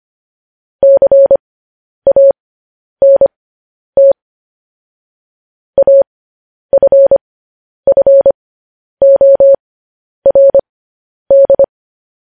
Morse code, heard as a keyed tone, about 25 wpm, spaced as for 10 wpm, 550 Hz. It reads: CANT AFFORD
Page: news-headlines-in-morse-code-at-10-wpm